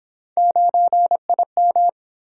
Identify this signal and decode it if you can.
Morse code, heard as a keyed tone, about 29 wpm, 700 Hz. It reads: 9IM